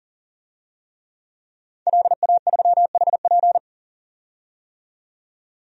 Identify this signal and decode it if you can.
Morse code, heard as a keyed tone, about 37 wpm, 700 Hz. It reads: LA3HP
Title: DXCC-100 Call Signs - 5 Characters - 40wpm